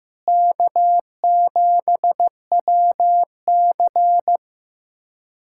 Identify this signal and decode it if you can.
Morse code, heard as a keyed tone, about 15 wpm, 700 Hz. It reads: K7WC
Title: USA Callsign DE Callsign - 4 Characters - 15wpm